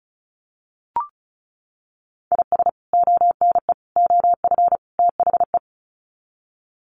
Morse code, heard as a keyed tone, about 34 wpm, 700 Hz. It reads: IS ONE OF THE
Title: Sets of 4 Words 35wpm